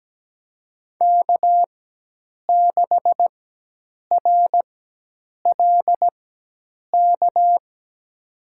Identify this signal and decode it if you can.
Morse code, heard as a keyed tone, about 17 wpm, spaced as for 8 wpm, 700 Hz. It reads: K6RLK
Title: USA Call Signs - 5 Characters - 4x Character Spacing - 17wpm